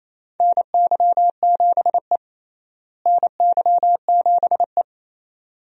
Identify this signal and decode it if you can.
Morse code, heard as a keyed tone, about 28 wpm, 700 Hz. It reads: NY7E NY7E
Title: USA Call Signs - 4 Characters - Repeated 2x - 28wpm